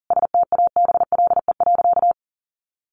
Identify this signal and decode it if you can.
Morse code, heard as a keyed tone, about 40 wpm, 700 Hz. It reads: STABLE.
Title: Sentences from Top 1500 Words 40wpm